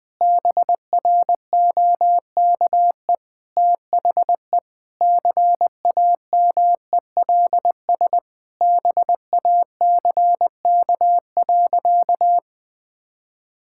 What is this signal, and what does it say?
Morse code, heard as a keyed tone, about 20 wpm, 700 Hz. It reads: BROKE THE CAMELS BACK.
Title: Sentences with 7-Word Phrases - 20wpm